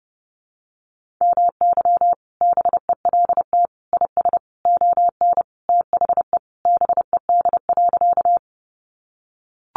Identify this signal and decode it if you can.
Morse code, heard as a keyed tone, about 30 wpm, 700 Hz. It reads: MY BELT IS ON THE BED.
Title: Sentences from Top 1400 Words 30wpm